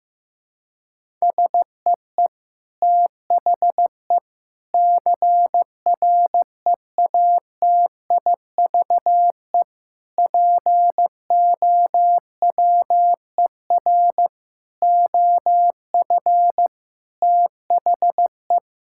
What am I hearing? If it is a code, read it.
Morse code, heard as a keyed tone, about 15 wpm, 700 Hz. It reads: SEE THE CREATIVE POWER OF THE
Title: Sentences from Top 1000 Words - Review Part 5 - 15wpm